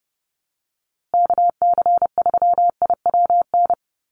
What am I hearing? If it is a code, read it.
Morse code, heard as a keyed tone, about 30 wpm, 700 Hz. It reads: KC3IWN
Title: USA Call Signs Repeated 2x 30wpm